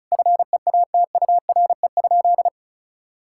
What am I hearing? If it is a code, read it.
Morse code, heard as a keyed tone, about 36 wpm, 700 Hz. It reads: FEATURE?
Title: Sentences from Top 1000 Words - Review Part 5 - 35wpm